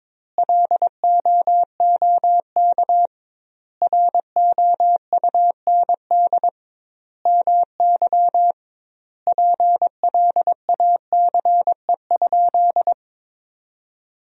Morse code, heard as a keyed tone, about 21 wpm, 700 Hz. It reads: LOOK ROUND MY PLACE?
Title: 2x Word-Spacing - Sentences from Top 200 Words - 22wpm